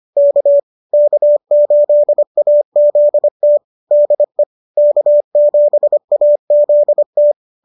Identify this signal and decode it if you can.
Morse code, heard as a keyed tone, about 25 wpm, 590 Hz. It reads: K K8AZT DE K7AZT